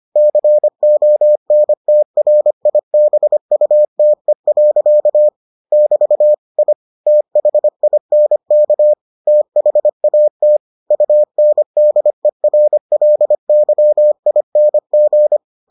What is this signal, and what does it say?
Morse code, heard as a keyed tone, about 25 wpm, 600 Hz. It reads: CONTRIBUTE. <BT> I THINK THAT UNDERLYING